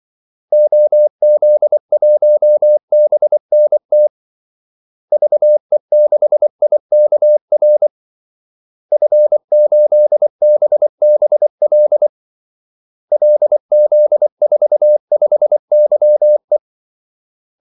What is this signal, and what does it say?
Morse code, heard as a keyed tone, about 24 wpm, 600 Hz. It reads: OZ1BNT VE6IKR F8BBL LZ45YE
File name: Callsigns-24wpm